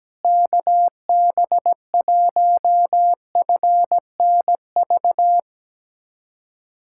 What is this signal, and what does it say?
Morse code, heard as a keyed tone, about 17 wpm, 700 Hz. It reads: KB1FNV
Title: USA Callsign DE Callsign - 6 Characters - 17wpm